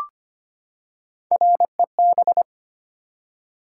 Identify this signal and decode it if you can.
Morse code, heard as a keyed tone, about 25 wpm, 700 Hz. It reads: REB
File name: Three Letter-Number Combinations - 25wpm